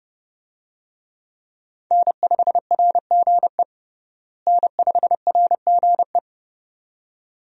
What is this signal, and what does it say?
Morse code, heard as a keyed tone, about 30 wpm, 700 Hz. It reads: N5RGE N5RGE
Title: USA Call Signs - 5 Characters - Repeated 2x - 30wpm